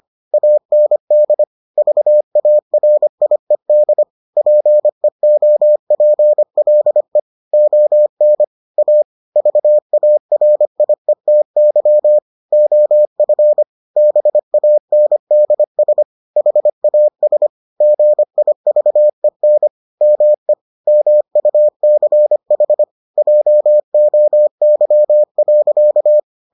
Morse code, heard as a keyed tone, about 25 wpm, 600 Hz. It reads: AND VARIED PEOPLE ON A VARIETY OF BANDS HAS GIVEN ME MUCH JOY.